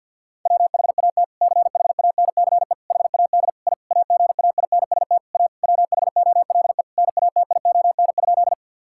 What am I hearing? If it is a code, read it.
Morse code, heard as a keyed tone, about 50 wpm, 700 Hz. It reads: WHAT CHANCE HAD I AGAINST A WHOLE NATION?
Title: Sentences from Top 1000 Words - Review Part 6 - 50wpm